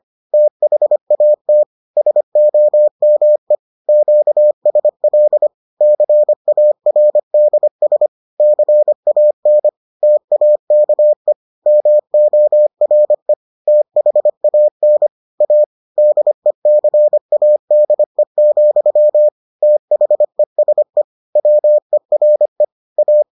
Morse code, heard as a keyed tone, about 25 wpm, 600 Hz. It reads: THAT SOME QSL CARDS CAN TAKE MORE THAN A DECADE, THESE WERE A